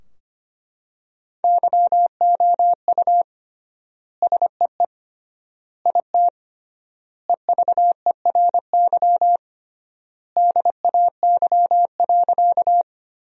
Morse code, heard as a keyed tone, about 25 wpm, 700 Hz. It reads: YOU SEE IT EVERY DAY.
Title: 3x Word-Spacing - Sentences from Top 200 Words - 25wpm